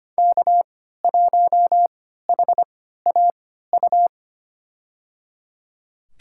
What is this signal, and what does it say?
Morse code, heard as a keyed tone, about 25 wpm, spaced as for 14 wpm, 700 Hz. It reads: K1HAU
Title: USA Call Signs - 5 Characters - 3x Character Spacing - 25wpm